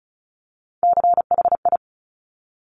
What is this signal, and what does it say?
Morse code, heard as a keyed tone, about 36 wpm, 700 Hz. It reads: CHI
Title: Top Three Letter Combinations in Words 35wpm